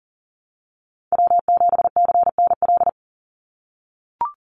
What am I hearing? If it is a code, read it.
Morse code, heard as a keyed tone, about 40 wpm, 700 Hz. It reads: W7CNL
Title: NA SOTA Chaser Callsigns 40wpm